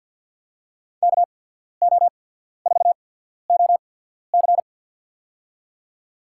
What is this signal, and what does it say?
Morse code, heard as a keyed tone, about 52 wpm, spaced as for 14 wpm, 700 Hz. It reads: KO4OC
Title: USA Call Signs - 5 Characters - 8x Character Spacing - 50wpm